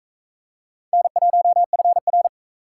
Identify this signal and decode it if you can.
Morse code, heard as a keyed tone, about 41 wpm, 700 Hz. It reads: N1FR